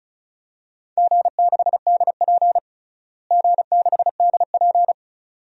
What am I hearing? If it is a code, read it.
Morse code, heard as a keyed tone, about 36 wpm, 700 Hz. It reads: G6DP G6DP
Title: DXCC-100 Call Signs - 4 Characters - Repeated 2x - 35wpm